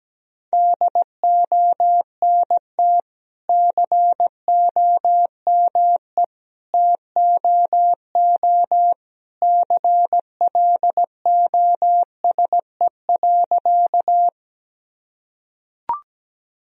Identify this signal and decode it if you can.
Morse code, heard as a keyed tone, about 17 wpm, 700 Hz. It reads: DONT COME TOO CLOSE.
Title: Sentences from Top 1000 Words - Review Part 1 - 17wpm